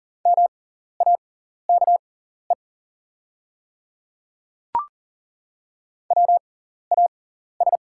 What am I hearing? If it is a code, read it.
Morse code, heard as a keyed tone, about 37 wpm, spaced as for 14 wpm, 690 Hz. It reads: MAKE WAS
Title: 6x Character Spacing - Top 100 Words - 40wpm